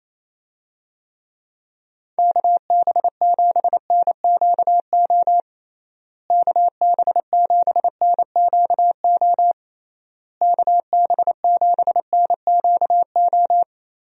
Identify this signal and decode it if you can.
Morse code, heard as a keyed tone, about 28 wpm, 700 Hz. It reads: KB7NQO KB7NQO KB7NQO
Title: USA Call Signs - 6 Characters - Repeated 3x - 28wpm